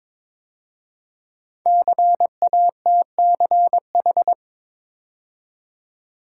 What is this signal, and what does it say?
Morse code, heard as a keyed tone, about 23 wpm, 700 Hz. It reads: CATCH
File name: 5 Letter Words - First 3 Letters Given - 22wpm